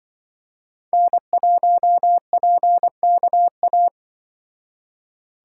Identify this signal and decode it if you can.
Morse code, heard as a keyed tone, about 24 wpm, 700 Hz. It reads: N1PKA
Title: Speed Racing - USA Call Signs 20wpm